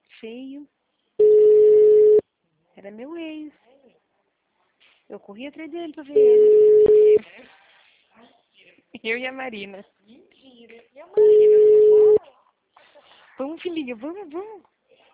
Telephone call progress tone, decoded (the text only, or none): ringback tone